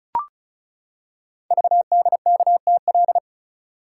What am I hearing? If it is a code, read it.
Morse code, heard as a keyed tone, about 36 wpm, 690 Hz. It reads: VDKTL
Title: Code Groups - 5 Letters - 35wpm